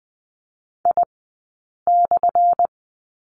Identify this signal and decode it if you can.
Morse code, heard as a keyed tone, about 20 wpm, spaced as for 8 wpm, 700 Hz. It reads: I/
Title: 177 Lesson 29d - Slash - Two Characters 20wpm